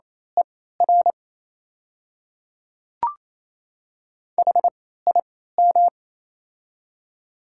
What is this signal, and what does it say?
Morse code, heard as a keyed tone, about 28 wpm, spaced as for 16 wpm, 700 Hz. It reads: ER HIM
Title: 3x Character Spacing - Top 200 Words - 28wpm